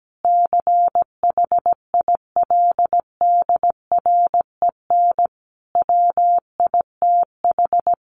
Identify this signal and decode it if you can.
Morse code, heard as a keyed tone, about 17 wpm, 700 Hz. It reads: CHILDREN WITH
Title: Sentences from Top 1400 Words 17wpm